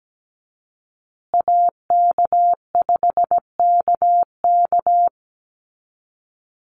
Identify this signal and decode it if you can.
Morse code, heard as a keyed tone, about 17 wpm, 700 Hz. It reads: AK5KK
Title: USA Call Signs Repeated 2x 17wpm